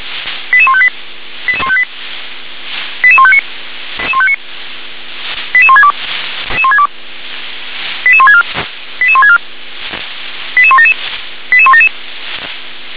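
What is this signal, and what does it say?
Five-tone selective call (ZVEI), heard as quick runs of five tones, tones of about 70 ms. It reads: 88158 88158 88159 88162 88164 88164 88177 88177
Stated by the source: ZVEI five-tone selective call received on SDR.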